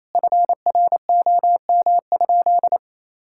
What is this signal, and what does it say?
Morse code, heard as a keyed tone, about 28 wpm, 700 Hz. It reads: FROM?
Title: Sentences from Top 1000 Words - Review Part 5 - 28wpm